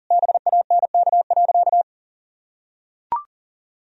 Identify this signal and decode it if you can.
Morse code, heard as a keyed tone, about 40 wpm, 700 Hz. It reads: BANK.
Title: Sentences from Top 1000 Words - Review Part 4 - 40wpm